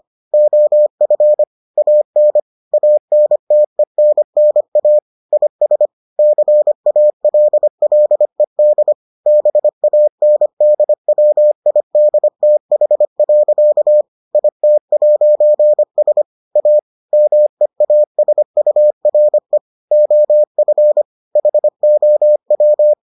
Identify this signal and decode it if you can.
Morse code, heard as a keyed tone, about 25 wpm, 600 Hz. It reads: OF AN ANTENNA IS CALLED BANDWIDTH. IT'S A MEASURE OF HOW